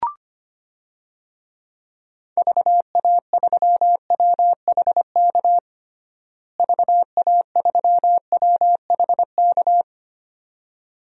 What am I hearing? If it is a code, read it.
Morse code, heard as a keyed tone, about 25 wpm, 700 Hz. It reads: VA3WHK VA3WHK
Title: DXCC-100 Call Signs - 6 Characters - Repeated 2x - 25wpm